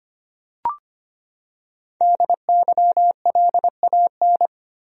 Morse code, heard as a keyed tone, about 25 wpm, 700 Hz. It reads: DYLAN
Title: USA Common Names - Max 5 Letters - 25wpm